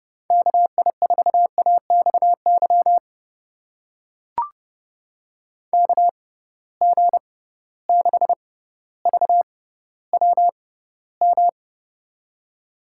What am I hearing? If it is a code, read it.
Morse code, heard as a keyed tone, about 31 wpm, spaced as for 16 wpm, 700 Hz. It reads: KI4AXY KG6VWM
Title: USA Call Signs - 6 Characters - 6x Character Spacing - 30wpm